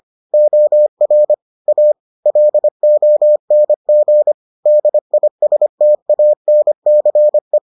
Morse code, heard as a keyed tone, about 25 wpm, 600 Hz. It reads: OR A LONG DISTANCE